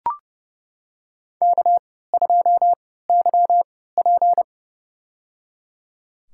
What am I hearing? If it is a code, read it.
Morse code, heard as a keyed tone, about 30 wpm, spaced as for 17 wpm, 700 Hz. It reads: K2YP
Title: USA Call Signs - 4 Characters - 3x Character Spacing - 30wpm